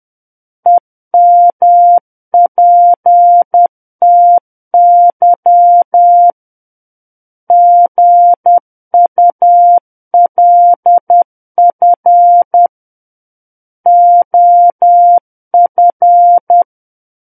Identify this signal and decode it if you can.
Morse code, heard as a keyed tone, about 10 wpm, 700 Hz. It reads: EMPTY GULF OF
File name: Book One / 01-Chapter 1 - 10 WPM